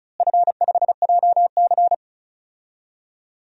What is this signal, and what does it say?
Morse code, heard as a keyed tone, about 34 wpm, 700 Hz. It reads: F5JC
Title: DXCC-100 Call Signs - 4 Characters - Repeated 3x - 35wpm